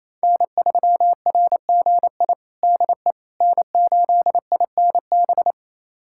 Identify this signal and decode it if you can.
Morse code, heard as a keyed tone, about 28 wpm, 700 Hz. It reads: N3RGI DE N8INB
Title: USA Callsign DE Callsign - 5 Characters - 28wpm